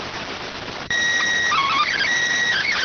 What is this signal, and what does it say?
SSTV (slow-scan television), PD 290.